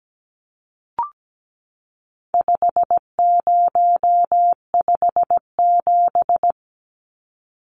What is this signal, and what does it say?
Morse code, heard as a keyed tone, about 17 wpm, 700 Hz. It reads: 5057
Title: Numbers - 4-Digit 17wpm